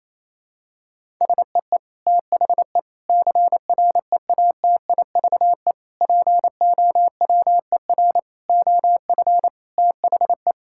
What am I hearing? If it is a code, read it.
Morse code, heard as a keyed tone, about 28 wpm, 700 Hz. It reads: SEE THE CREATIVE POWER OF THE
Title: Sentences from Top 1000 Words - Review Part 5 - 28wpm